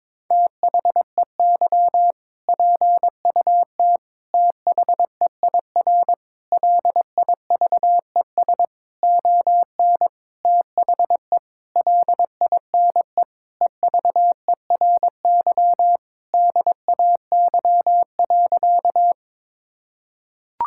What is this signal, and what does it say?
Morse code, heard as a keyed tone, about 21 wpm, 700 Hz. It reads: THEY PUT THEIR LIVES ON THE LINE EVERY DAY.
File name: practice / 22wpm / Sentences with 8-Word Phrases - 22wpm